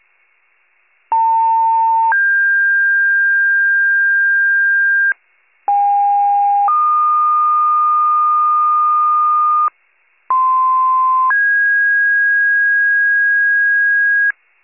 Two-tone page, heard as a one-second tone, then a three-second tone, about 900 then 1600 Hz, about 810 then 1220 Hz, about 1040 then 1650 Hz, with noise. It is heard from 1.1 to 14.3 s.